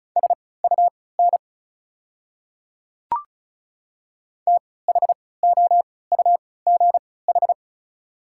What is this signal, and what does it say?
Morse code, heard as a keyed tone, about 36 wpm, spaced as for 20 wpm, 690 Hz. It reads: SUN THOUGH
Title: 3x Character Spacing - Top 500 Words - 35wpm